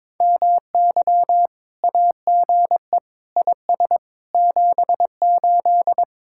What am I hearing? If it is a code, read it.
Morse code, heard as a keyed tone, about 23 wpm, 700 Hz. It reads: MY AGE IS 78
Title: Speed-Racing Common QSO Phrases Encore - 22wpm